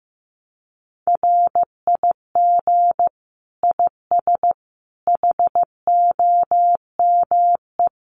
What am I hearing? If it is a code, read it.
Morse code, heard as a keyed tone, about 15 wpm, 700 Hz. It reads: RIG IS HOME
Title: Common QSO Phrases 15wpm